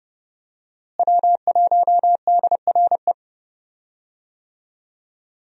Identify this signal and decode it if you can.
Morse code, heard as a keyed tone, about 30 wpm, 700 Hz. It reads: W1DRE
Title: USA Call Signs - 5 Characters - Repeated 2x - 30wpm